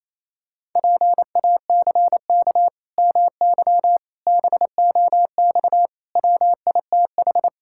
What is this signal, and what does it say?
Morse code, heard as a keyed tone, about 28 wpm, 700 Hz. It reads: PACK MY BOX WITH
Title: Pangram Challenge v2 - 28wpm